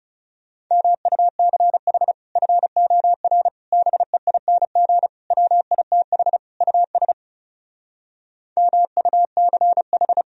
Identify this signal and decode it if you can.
Morse code, heard as a keyed tone, about 40 wpm, 700 Hz. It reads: MUCH FOR BEING WITH US MUCH